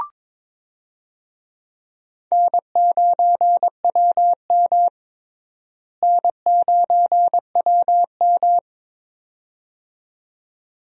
Morse code, heard as a keyed tone, about 22 wpm, 700 Hz. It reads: N9WM N9WM
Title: USA Call Signs - 4 Characters - Repeated 2x - 22wpm